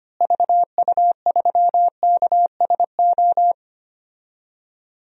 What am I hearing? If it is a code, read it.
Morse code, heard as a keyed tone, about 25 wpm, 700 Hz. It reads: VU3KSO
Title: DXCC-100 Call Signs - 6 Characters - Repeated 2x - 25wpm